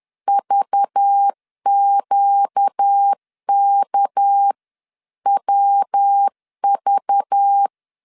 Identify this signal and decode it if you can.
Morse code, heard as a keyed tone, about 10 wpm, 800 Hz. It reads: VQKWV